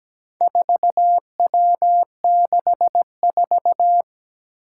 Morse code, heard as a keyed tone, about 17 wpm, 700 Hz. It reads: 4W64
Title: Four Challenging Letter-Number Combinations 17wpm